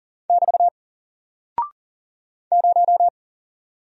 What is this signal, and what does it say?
Morse code, heard as a keyed tone, about 40 wpm, 700 Hz. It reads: <BT> 0
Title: Single Character (Letter:Number:Punctuation:Prosign) Alt 40wpm